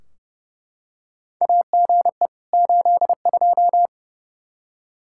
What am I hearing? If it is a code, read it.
Morse code, heard as a keyed tone, about 30 wpm, 700 Hz. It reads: AGE 82
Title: Practice QSO - Exchange 3 - More Info - 30wpm